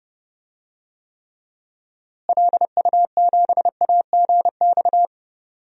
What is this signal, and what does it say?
Morse code, heard as a keyed tone, about 30 wpm, 700 Hz. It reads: LU7AGX
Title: DXCC-100 Call Signs - 6 Characters - Repeated 3x - 30wpm